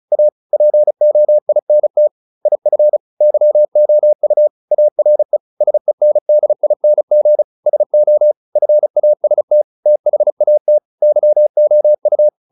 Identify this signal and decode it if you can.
Morse code, heard as a keyed tone, about 34 wpm, 600 Hz. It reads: A POINT IF YOU ARE SENDING SO FAST THAT YOU